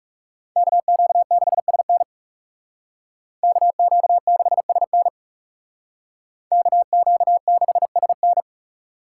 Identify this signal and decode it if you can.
Morse code, heard as a keyed tone, about 37 wpm, 700 Hz. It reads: KQ6SN KQ6SN KQ6SN
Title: Speed Racing - USA 2x2 Call Signs 30wpm